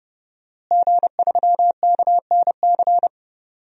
Morse code, heard as a keyed tone, about 31 wpm, 700 Hz. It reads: G3KNC
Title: DXCC-100 Call Signs - 5 Characters - Repeated 3x - 30wpm